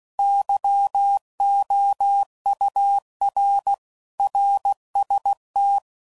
Morse code, heard as a keyed tone, about 16 wpm, 800 Hz. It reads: YOUR RST
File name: CW Code Groups @15WPM / RSTs / Your RSTs